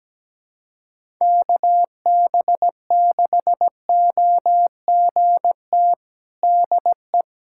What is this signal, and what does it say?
Morse code, heard as a keyed tone, about 17 wpm, 700 Hz. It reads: KB6OGT DE
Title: USA Callsign DE Callsign - 6 Characters - 17wpm